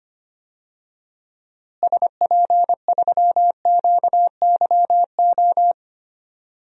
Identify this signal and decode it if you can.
Morse code, heard as a keyed tone, about 25 wpm, 700 Hz. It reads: SP3QYO
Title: DXCC-100 Call Signs - 6 Characters - Repeated 3x - 25wpm